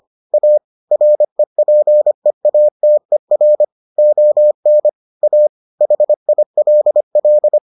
Morse code, heard as a keyed tone, about 25 wpm, 600 Hz. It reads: A REPEATER ON A HILL